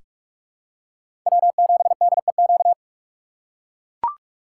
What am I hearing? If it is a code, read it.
Morse code, heard as a keyed tone, about 44 wpm, 700 Hz. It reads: W7DEQ
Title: USA Call Signs - 5 Characters - 6x Character Spacing - 45wpm